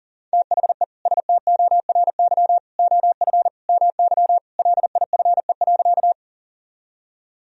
Morse code, heard as a keyed tone, about 40 wpm, 700 Hz. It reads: THE STORY OF MY LIFE.
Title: Sentences from Top 1000 Words - Review Part 1 - 40wpm